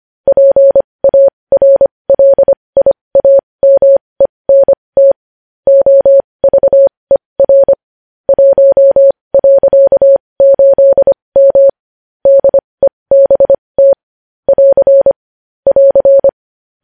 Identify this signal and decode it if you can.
Morse code, heard as a keyed tone, about 25 wpm, spaced as for 20 wpm, 550 Hz. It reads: PARLIAMENT OVER 1.8M DEBT <AR> <AR>
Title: News Headlines at 20